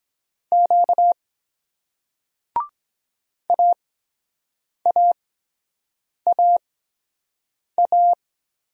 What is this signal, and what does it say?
Morse code, heard as a keyed tone, about 23 wpm, spaced as for 7 wpm, 700 Hz. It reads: Q AAAA